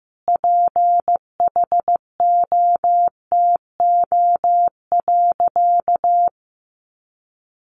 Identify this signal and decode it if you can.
Morse code, heard as a keyed tone, about 15 wpm, 700 Hz. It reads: PHOTO.
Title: Sentences from Top 1200 Words 15wpm